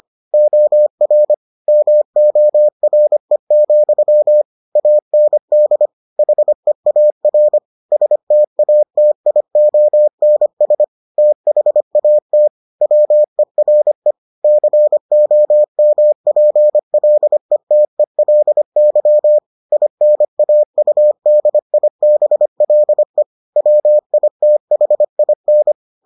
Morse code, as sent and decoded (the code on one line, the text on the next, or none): --- .-. / -- --- .-. --..-- / .- -. -.. / .... .- .-. / ... - .- - .. --- -. ... / - .... .- - / .-- . .-. / -.-. --- -- .--. .-.. - . .-.. -.-- / .. -. .- ..- -.. .. -... .-.. / .-- .. - .... .. -.
OR MORE, AND HEAR STATIONS THAT WERE COMPLETELY INAUDIBLE WITHIN